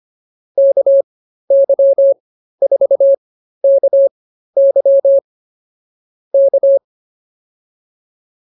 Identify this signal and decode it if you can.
Morse code, heard as a keyed tone, about 24 wpm, spaced as for 13 wpm, 550 Hz. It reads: KY4KY K
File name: QSO 108 13